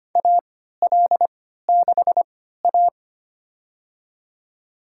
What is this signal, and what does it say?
Morse code, heard as a keyed tone, about 25 wpm, spaced as for 14 wpm, 700 Hz. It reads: AL6A
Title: USA Call Signs - 4 Characters - 3x Character Spacing - 25wpm